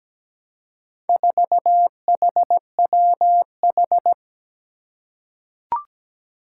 Morse code, heard as a keyed tone, about 17 wpm, 700 Hz. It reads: 4HWH
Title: Four Challenging Letter-Number Combinations 17wpm